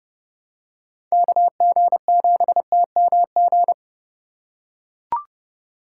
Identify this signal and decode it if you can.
Morse code, heard as a keyed tone, about 30 wpm, 700 Hz. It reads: KG7TMG
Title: USA Call Signs - 6 Characters - 2x Character Spacing - 30wpm